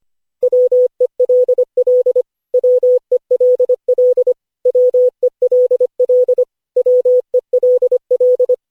Morse code, heard as a keyed tone, about 25 wpm, 500 Hz. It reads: WELL WELL WELL WELL
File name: cw / Session5 cw training